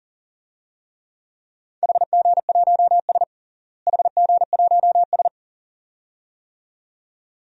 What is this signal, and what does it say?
Morse code, heard as a keyed tone, about 40 wpm, 700 Hz. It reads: HG1S HG1S
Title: DXCC-100 Call Signs - 4 Characters - Repeated 2x - 40wpm